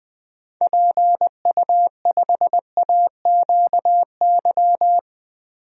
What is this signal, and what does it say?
Morse code, heard as a keyed tone, about 20 wpm, 700 Hz. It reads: PU5AQY
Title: DXCC-100 Call Signs - 6 Characters - Repeated 2x - 20wpm